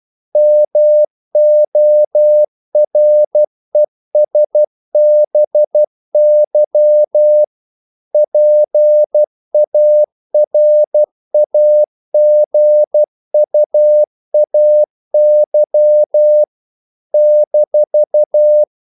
Morse code, heard as a keyed tone, about 12 wpm, 600 Hz. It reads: MORESBY PARAGUAY -
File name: CW / Practice / NationalCapitals12